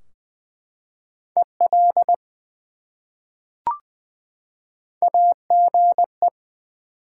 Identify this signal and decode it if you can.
Morse code, heard as a keyed tone, about 20 wpm, 700 Hz. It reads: EL AGE
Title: Top 180 QSO Elements 20wpm